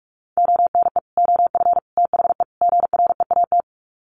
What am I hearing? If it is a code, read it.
Morse code, heard as a keyed tone, about 44 wpm, 700 Hz. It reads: ONE OF THE GREAT
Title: Sets of 4 Words 45wpm